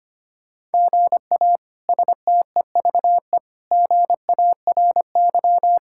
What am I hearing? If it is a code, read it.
Morse code, heard as a keyed tone, about 25 wpm, 700 Hz. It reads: GA STEVE GARY
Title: K1USN SST QSO Practice - 25wpm